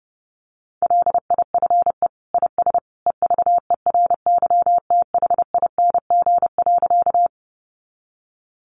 Morse code, heard as a keyed tone, about 30 wpm, 700 Hz. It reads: LIFE IS EVERYTHING.
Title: Sentences from Top 700 Words 30wpm